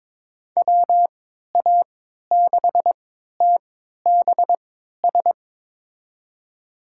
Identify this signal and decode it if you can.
Morse code, heard as a keyed tone, about 22 wpm, spaced as for 13 wpm, 700 Hz. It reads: WA6TBS
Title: USA Call Signs - 6 Characters - 3x Character Spacing - 22wpm